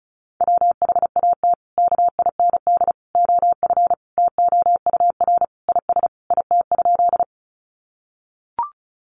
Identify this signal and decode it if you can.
Morse code, heard as a keyed tone, about 34 wpm, 700 Hz. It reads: WHAT KIND OF TOUR IS IT?
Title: Sentences from Top 1400 Words 35wpm